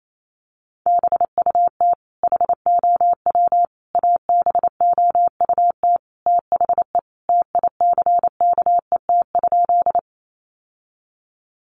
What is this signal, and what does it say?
Morse code, heard as a keyed tone, about 28 wpm, 700 Hz. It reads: BUT HOW ABOUT THE TICKET?
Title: Sentences from Top 1400 Words 28wpm